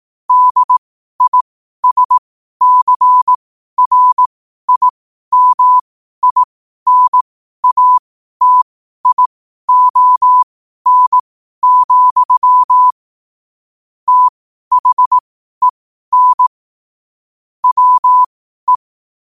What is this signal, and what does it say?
Morse code, heard as a keyed tone, about 18 wpm, spaced as for 12 wpm, 1000 Hz. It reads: DISCRIMINATION, THEN WE
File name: qotdfiles / 12wpm